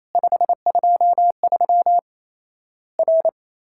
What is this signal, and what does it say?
Morse code, heard as a keyed tone, about 28 wpm, 700 Hz. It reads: 523 R
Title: Worked All Europe DX Contest - QTC Exchange - 28wpm